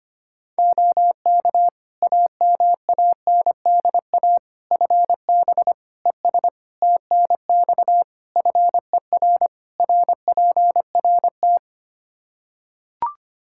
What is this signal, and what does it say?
Morse code, heard as a keyed tone, about 25 wpm, 700 Hz. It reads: OK AMANDA FB ES TNX FER RPRT
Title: K1USN SST QSO Practice 25wpm